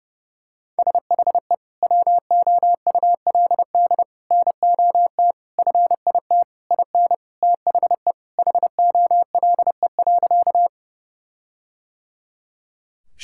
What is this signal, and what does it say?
Morse code, heard as a keyed tone, about 30 wpm, 700 Hz. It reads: SHE WOULD NOT FIT IN THE HOLE.
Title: Sentences from Top 1600 Words 30wpm